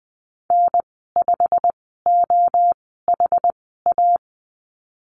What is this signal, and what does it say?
Morse code, heard as a keyed tone, about 20 wpm, spaced as for 15 wpm, 700 Hz. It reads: N5OHA
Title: 147 Lesson 24g - Letter y - Call Signs 20wpm